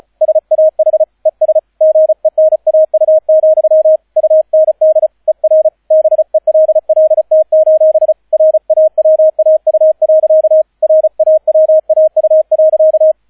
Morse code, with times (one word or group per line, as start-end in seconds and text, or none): 0.2-1.0 SAH
1.3-1.6 ES
1.8-4.0 GENAU,
4.2-5.1 UND
5.3-5.7 ER
5.9-8.1 BELLT:
8.3-10.6 RAWAU.
10.8-13.1 RAWAU.